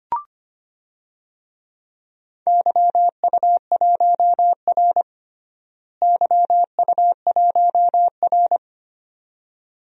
Morse code, heard as a keyed tone, about 25 wpm, 700 Hz. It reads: YU1R YU1R
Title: DXCC-100 Call Signs - 4 Characters - Repeated 2x - 25wpm